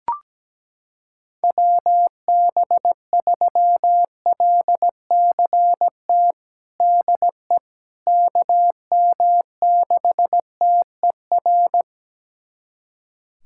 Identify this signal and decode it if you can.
Morse code, heard as a keyed tone, about 17 wpm, 700 Hz. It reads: WB3LCT DE KM6TER
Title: USA Callsign DE Callsign 17wpm